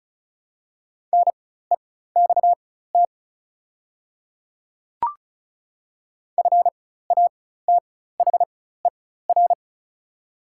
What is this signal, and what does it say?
Morse code, heard as a keyed tone, about 34 wpm, spaced as for 16 wpm, 700 Hz. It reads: NEXT FATHER
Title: 4x Character Spacing - Top 300 Words - 35wpm